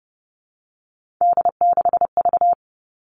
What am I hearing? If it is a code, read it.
Morse code, heard as a keyed tone, about 30 wpm, 700 Hz. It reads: D6V
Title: Three Challenging Letter-Number Combinations 30wpm